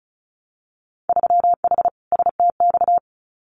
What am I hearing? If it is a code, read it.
Morse code, heard as a keyed tone, about 36 wpm, 700 Hz. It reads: 3H STX
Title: Winter Field Day Exchanges 35wpm